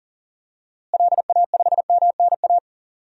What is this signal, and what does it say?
Morse code, heard as a keyed tone, about 40 wpm, 700 Hz. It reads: LA5MNA
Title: DXCC-100 Call Signs - 6 Characters - Repeated 2x - 40wpm